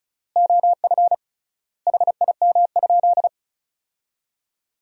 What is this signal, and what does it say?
Morse code, heard as a keyed tone, about 36 wpm, 700 Hz. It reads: OF HIM?
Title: 3x Word-Spacing - Sentences from Top 300 Words - 35wpm